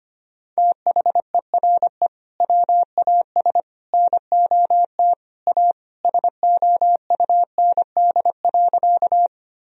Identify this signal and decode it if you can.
Morse code, heard as a keyed tone, about 25 wpm, 700 Hz. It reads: THERE WAS NOT A SOUND.
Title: Sentences from Top 1000 Words - Review Part 1 - 25wpm